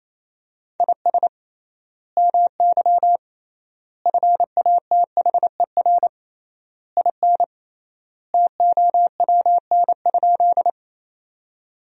Morse code, heard as a keyed tone, about 28 wpm, 700 Hz. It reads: IS MY FATHER IN TOWN?
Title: 3x Word-Spacing - Sentences from Top 500 Words - 28wpm